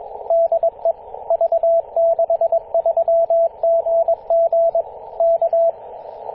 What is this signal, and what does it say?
Morse code, heard as a keyed tone, about 21 wpm, 670 Hz, with radio noise. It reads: DE V63GG K
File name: posts / 2016-apr-24-1040z-v63gg-14-016mhz-cw